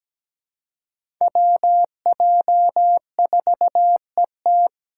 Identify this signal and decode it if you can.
Morse code, heard as a keyed tone, about 17 wpm, 700 Hz. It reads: WJ4ET
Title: USA Call Signs - 5 Characters - 2x Character Spacing - 17wpm